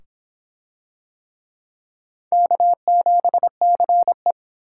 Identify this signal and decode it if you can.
Morse code, heard as a keyed tone, about 26 wpm, 700 Hz. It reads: K7CE